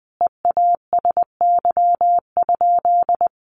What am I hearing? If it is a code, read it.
Morse code, heard as a keyed tone, about 19 wpm, 700 Hz. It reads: EASY?
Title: Sentences from Top 1500 Words 20wpm